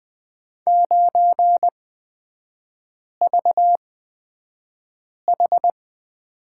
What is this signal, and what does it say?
Morse code, heard as a keyed tone, about 20 wpm, spaced as for 5 wpm, 700 Hz. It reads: 9VH
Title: 255 Lesson 41e - Period - Three Characters 20wpm